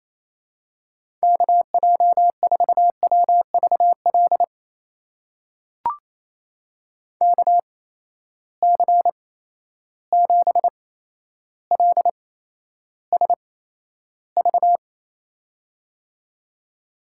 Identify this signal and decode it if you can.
Morse code, heard as a keyed tone, about 28 wpm, spaced as for 12 wpm, 700 Hz. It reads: KJ4WVL KC7LSV
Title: USA Call Signs - 6 Characters - 8x Character Spacing - 28wpm